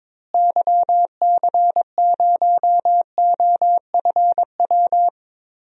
Morse code, heard as a keyed tone, about 21 wpm, 700 Hz. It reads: YC0OFW
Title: DXCC-100 Call Signs - 6 Characters - Repeated 3x - 22wpm